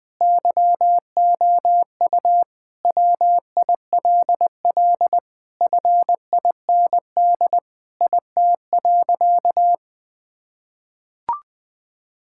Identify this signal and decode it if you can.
Morse code, heard as a keyed tone, about 20 wpm, 700 Hz. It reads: YOU WILL FIND IT.
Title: Sentences with 4-Word Phrases - 20wpm